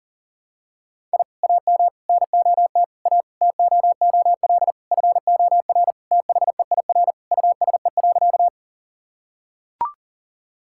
Morse code, heard as a keyed tone, about 40 wpm, 700 Hz. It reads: I AM NOT A TOOL FOR THEIR USE.
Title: Sentences from Top 1000 Words - Review Part 6 - 40wpm